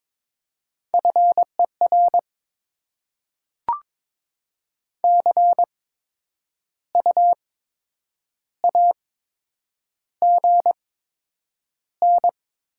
Morse code, heard as a keyed tone, about 22 wpm, spaced as for 8 wpm, 700 Hz. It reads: FER CU AGN
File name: Top 20 QSO Elements - 8x Character Spacing - 22wpm